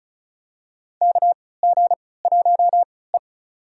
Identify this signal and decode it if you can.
Morse code, heard as a keyed tone, about 36 wpm, spaced as for 20 wpm, 700 Hz. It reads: KG1E
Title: USA Call Signs - 4 Characters - 3x Character Spacing - 35wpm